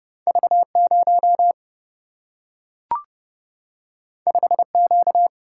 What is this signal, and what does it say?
Morse code, heard as a keyed tone, about 30 wpm, 700 Hz. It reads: V 0 5 Q
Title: Single-Letter Single-Number 30wpm